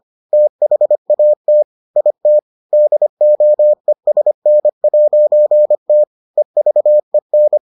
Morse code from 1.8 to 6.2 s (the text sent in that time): IT DOESN'T